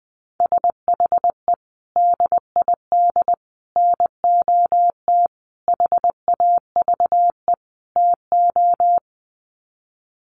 Morse code, heard as a keyed tone, about 20 wpm, 700 Hz. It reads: SHE DID NOT HAVE TO
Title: Sets of 5 Words Encore2 20wpm